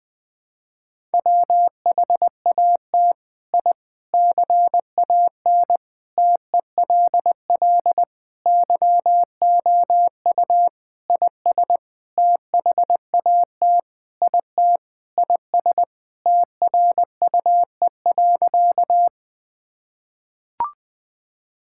Morse code, heard as a keyed tone, about 21 wpm, 700 Hz. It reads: WHAT I CAN TELL YOU IS THAT IT IS TRUE.
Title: Sentences with 7-Word Phrases - Spoken Beginning - 20wpm